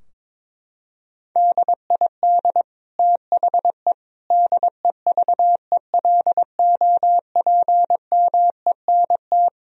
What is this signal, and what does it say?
Morse code, heard as a keyed tone, about 22 wpm, 700 Hz. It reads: DID THE DEVELOPMENT
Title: Sentences from Top 1000 Words - Review Part 4 - 22wpm